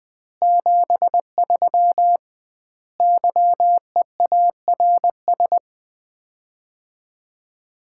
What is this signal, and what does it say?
Morse code, heard as a keyed tone, about 21 wpm, 700 Hz. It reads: 73 YEARS
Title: Common QSO Phrases - 2x Word-Spacing - 20wpm